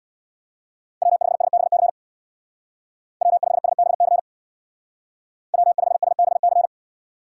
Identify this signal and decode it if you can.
Morse code, heard as a keyed tone, about 61 wpm, 700 Hz. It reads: W5SBC W5SBC W5SBC
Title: Speed Racing - USA 1x3 Call Signs 50wpm